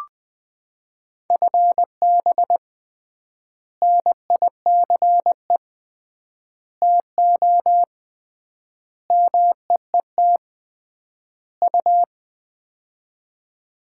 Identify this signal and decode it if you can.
Morse code, heard as a keyed tone, about 20 wpm, 700 Hz. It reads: FB NICE TO MEET U
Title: Common QSO Phrases - 3x Word-Spacing - 20wpm